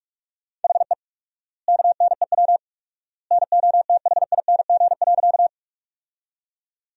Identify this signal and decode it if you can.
Morse code, heard as a keyed tone, about 41 wpm, 700 Hz. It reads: HE KNEW NOTHING.
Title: 4x Word-Spacing - Sentences from Top 500 Words - 45wpm